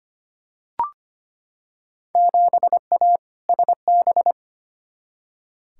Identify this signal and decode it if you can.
Morse code, heard as a keyed tone, about 25 wpm, 700 Hz. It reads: 7A SB
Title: Field Day Exchanges Encore - 25wpm